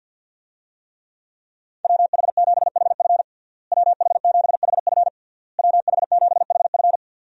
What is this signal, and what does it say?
Morse code, heard as a keyed tone, about 52 wpm, 700 Hz. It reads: WH7HF WH7HF WH7HF
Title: USA Call Signs - 5 Characters - Repeated 3x - 50wpm